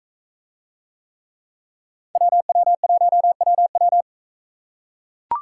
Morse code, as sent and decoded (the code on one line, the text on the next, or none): .-- .-- .---- .-- .--
WW1WW